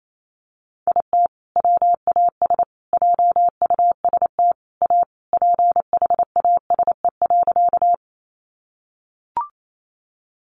Morse code, heard as a keyed tone, about 28 wpm, 700 Hz. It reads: IT WAS JUST A PHASE.
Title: Sentences from Top 1200 Words 28wpm